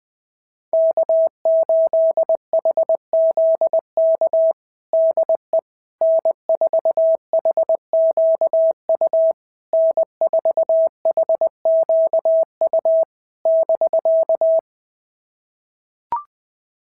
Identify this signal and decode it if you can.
Morse code, heard as a keyed tone, about 20 wpm, 650 Hz. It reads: K8HZK DE N4HQU N4HQU <BK>
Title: SOTA QSO Practice 20wpm